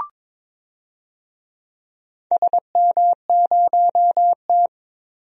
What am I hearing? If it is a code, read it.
Morse code, heard as a keyed tone, about 22 wpm, 700 Hz. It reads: SM0T
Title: DXCC-100 Call Signs - 4 Characters - 22wpm